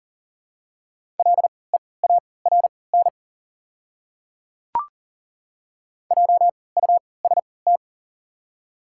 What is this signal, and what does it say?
Morse code, heard as a keyed tone, about 39 wpm, spaced as for 22 wpm, 700 Hz. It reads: LEARN JUST
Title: 3x Character Spacing - Top 300 Words - 40wpm